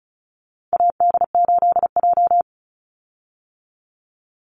Morse code, heard as a keyed tone, about 34 wpm, 700 Hz. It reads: AD8J
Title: Call Signs - 35wpm